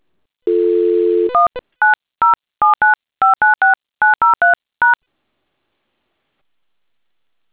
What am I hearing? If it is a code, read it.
DTMF, heard as telephone touch tones, tones of about 150 ms. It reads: 19079596903#